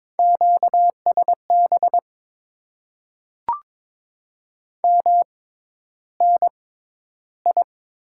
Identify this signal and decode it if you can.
Morse code, heard as a keyed tone, about 22 wpm, spaced as for 11 wpm, 700 Hz. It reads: QSB MNI
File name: Top 90 QSO Elements - 6x Character Spacing - 22wpm